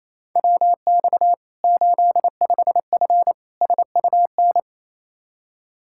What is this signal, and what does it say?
Morse code, heard as a keyed tone, about 28 wpm, 700 Hz. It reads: WX 85F SUN
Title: Practice QSO - Exchange 2 - Info - 28wpm